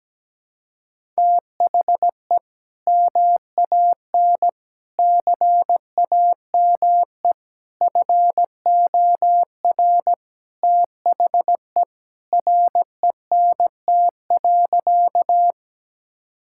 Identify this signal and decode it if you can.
Morse code, heard as a keyed tone, about 17 wpm, 700 Hz. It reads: THE MAN CAME FOR THE RENT.
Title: Sentences from Top 1000 Words - Review Part 5 - 17wpm